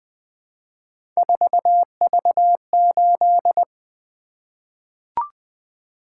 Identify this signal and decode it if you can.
Morse code, heard as a keyed tone, about 20 wpm, 700 Hz. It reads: 4V8
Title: Three Letter-Number Combinations - 20wpm